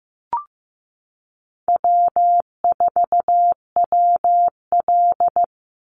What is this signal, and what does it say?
Morse code, heard as a keyed tone, about 15 wpm, 700 Hz. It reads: W4WL
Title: USA Call Signs 15wpm